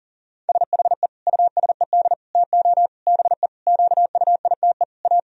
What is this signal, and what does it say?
Morse code, heard as a keyed tone, about 40 wpm, 700 Hz. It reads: SHE USED TO BE QUITE A
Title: Sentences from Top 1000 Words - Review Part 2 - 40wpm